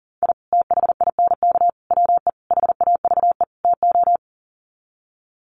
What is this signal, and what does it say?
Morse code, heard as a keyed tone, about 40 wpm, 700 Hz. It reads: I THINK WE HAVE TO
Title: Sets of 5 Words 40wpm